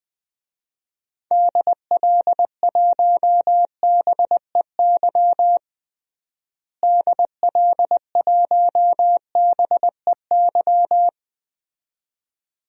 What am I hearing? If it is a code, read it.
Morse code, heard as a keyed tone, about 20 wpm, 700 Hz. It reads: DL1BEY DL1BEY
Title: DXCC-100 Call Signs - 6 Characters - Repeated 2x - 20wpm